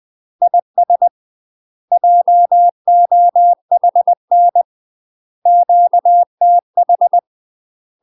Morse code, heard as a keyed tone, about 20 wpm, 700 Hz. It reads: IS JOHN QTH
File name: UK MASTER QSOs TRAINING / 20WPM / UK MASTER QSOs 0063